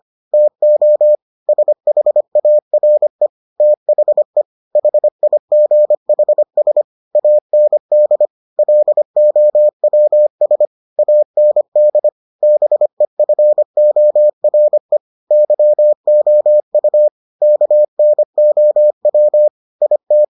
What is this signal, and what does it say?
Morse code, heard as a keyed tone, about 25 wpm, 600 Hz. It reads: TO SHARE THE HIGHS AND LOWS AND BEFORE YOU KNOW IT